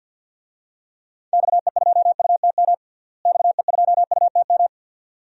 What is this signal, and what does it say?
Morse code, heard as a keyed tone, about 52 wpm, 700 Hz. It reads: XE2UTM XE2UTM
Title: DXCC-100 Call Signs - 6 Characters - Repeated 2x - 50wpm